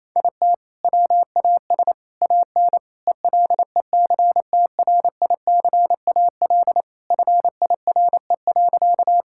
Morse code, heard as a keyed tone, about 28 wpm, 700 Hz. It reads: IT WAS AN ELECTRICAL FIRE.
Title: Sentences from Top 1000 Words - Review Part 6 - 28wpm